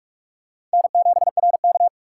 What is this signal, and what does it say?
Morse code, heard as a keyed tone, about 44 wpm, 700 Hz. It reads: N7RK